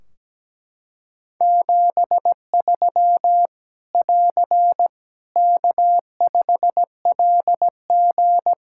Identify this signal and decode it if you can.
Morse code, heard as a keyed tone, about 17 wpm, 700 Hz. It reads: 73 <AR> K5LG
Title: K1USN SST QSO Practice 17wpm